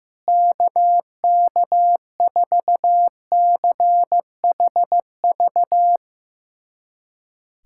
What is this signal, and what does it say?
Morse code, heard as a keyed tone, about 15 wpm, 700 Hz. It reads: KK4CHV
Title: USA Callsign DE Callsign 15wpm